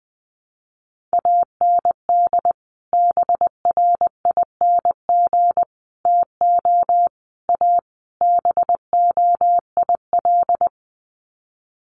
Morse code, heard as a keyed tone, about 20 wpm, 700 Hz. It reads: AND BRING TO A BOIL
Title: Sets of 5 Words Encore 20wpm